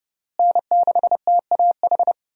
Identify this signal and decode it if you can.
Morse code, heard as a keyed tone, about 30 wpm, 700 Hz. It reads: N6TAH
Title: USA Call Signs - 5 Characters - Repeated 2x - 30wpm